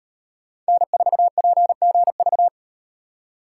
Morse code, heard as a keyed tone, about 38 wpm, 700 Hz. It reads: N4PGV